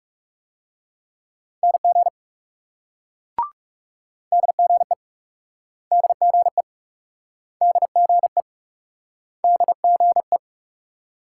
Morse code, heard as a keyed tone, about 40 wpm, 700 Hz. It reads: NG DGE DGE DGE DGE